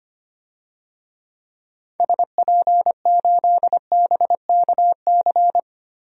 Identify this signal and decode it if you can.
Morse code, heard as a keyed tone, about 25 wpm, 700 Hz. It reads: SP8BKC
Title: DXCC-100 Call Signs - 6 Characters - Repeated 2x - 25wpm